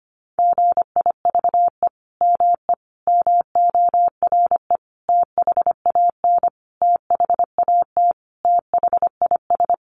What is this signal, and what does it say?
Morse code, heard as a keyed tone, about 25 wpm, 700 Hz. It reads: GIVE ME MORE THAN THAT THIS
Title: Sentences from Top 200 Words 25wpm